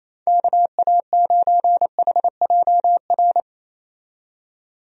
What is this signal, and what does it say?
Morse code, heard as a keyed tone, about 28 wpm, 700 Hz. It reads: KA9HJR
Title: USA Call Signs - 6 Characters - 28wpm